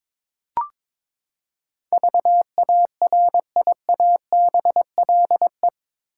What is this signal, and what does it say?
Morse code, heard as a keyed tone, about 22 wpm, 700 Hz. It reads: VARIABLE
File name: practice / 22wpm / Eight Letter Words - 22wpm